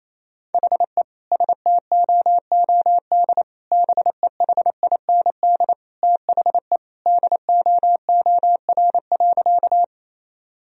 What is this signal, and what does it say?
Morse code, heard as a keyed tone, about 28 wpm, 700 Hz. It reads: HE STOOD BEHIND THE DOOR.
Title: Sentences from Top 1000 Words - Review Part 2 - 28wpm